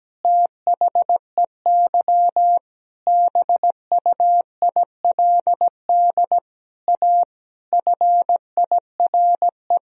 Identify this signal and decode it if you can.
Morse code, heard as a keyed tone, about 17 wpm, 700 Hz. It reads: THEY BUILD A FIRE
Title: Sentences from Top 1000 Words - Review Part 3 - 17wpm